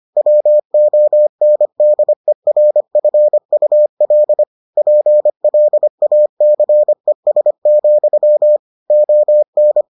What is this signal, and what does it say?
Morse code, heard as a keyed tone, about 25 wpm, 600 Hz. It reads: WONDERFUL PLACES, ON